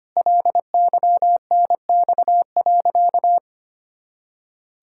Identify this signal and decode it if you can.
Morse code, heard as a keyed tone, about 25 wpm, 700 Hz. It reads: LYNX.
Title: Pangram Challenge v2 - 25wpm